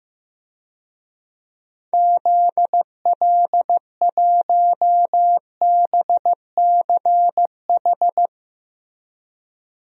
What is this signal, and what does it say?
Morse code, heard as a keyed tone, about 15 wpm, 700 Hz. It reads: ZL1BCH
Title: DXCC-100 Call Signs - 6 Characters - Repeated 2x - 15wpm